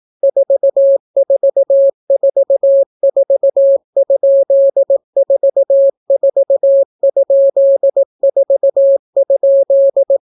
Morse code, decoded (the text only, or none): 4444?44?4?